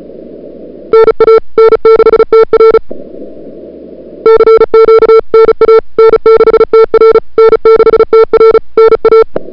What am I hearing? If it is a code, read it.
Morse code, heard as a keyed tone, about 35 wpm, 440 Hz. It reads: NA N6TR CQ NA N6TR N6TR NA
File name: CWSprint / February 2007 CW Sprint / 01-2007 CW Sprint Hour1